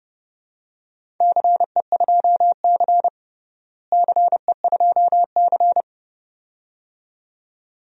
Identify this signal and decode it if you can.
Morse code, heard as a keyed tone, about 30 wpm, 700 Hz. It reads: CE2C CE2C
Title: DXCC-100 Call Signs - 4 Characters - Repeated 2x - 30wpm